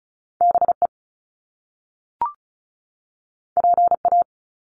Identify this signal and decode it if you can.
Morse code, heard as a keyed tone, about 36 wpm, 700 Hz. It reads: BE PA
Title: Top Two Letter Combinations in Words 35wpm